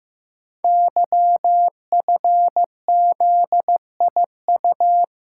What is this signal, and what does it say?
Morse code, heard as a keyed tone, about 15 wpm, 700 Hz. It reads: YFZIU
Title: Code Groups - 5 Letters - 15wpm